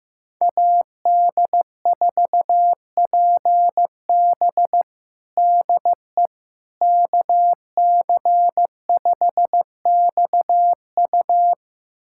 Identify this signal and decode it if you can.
Morse code, heard as a keyed tone, about 15 wpm, 700 Hz. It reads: AD4PB DE KC5XU
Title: USA Callsign DE Callsign - 5 Characters - 15wpm